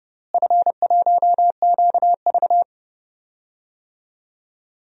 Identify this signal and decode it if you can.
Morse code, heard as a keyed tone, about 30 wpm, 700 Hz. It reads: F1QV
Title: DXCC-100 Call Signs - 4 Characters - Repeated 3x - 30wpm